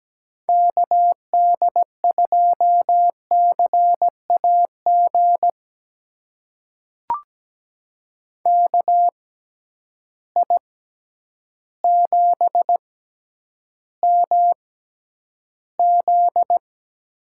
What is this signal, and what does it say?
Morse code, heard as a keyed tone, about 17 wpm, spaced as for 9 wpm, 700 Hz. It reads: KD2CAG KI7MZ
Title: USA Call Signs - 6 Characters - 6x Character Spacing - 17wpm